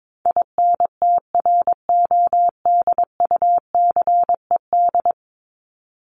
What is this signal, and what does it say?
Morse code, heard as a keyed tone, about 22 wpm, 700 Hz. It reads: INTRODUCED
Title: ED Suffix - Top 100 Words with Base Words 22wpm